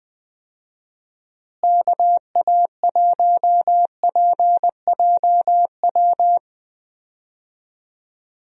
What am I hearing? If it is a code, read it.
Morse code, heard as a keyed tone, about 20 wpm, 700 Hz. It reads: KA1PJW
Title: USA Call Signs - 6 Characters - Repeated 3x - 20wpm